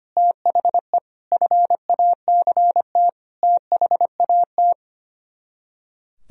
Morse code, heard as a keyed tone, about 33 wpm, 700 Hz. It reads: THE FACT THAT